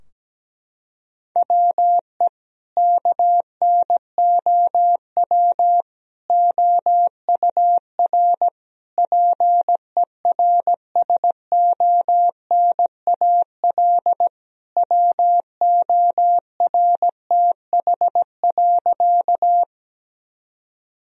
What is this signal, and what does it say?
Morse code, heard as a keyed tone, about 17 wpm, 700 Hz. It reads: WE KNOW OUR PERSONAL WORTH.
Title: Sentences from Top 1000 Words - Review Part 4 - 17wpm